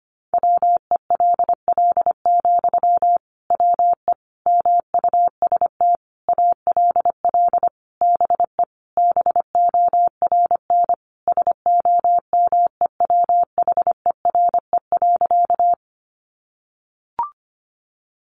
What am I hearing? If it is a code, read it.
Morse code, heard as a keyed tone, about 25 wpm, 700 Hz. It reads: WELL, WE MUST ALL BE BORN SOMEWHERE.
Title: Sentences from Top 1400 Words 25wpm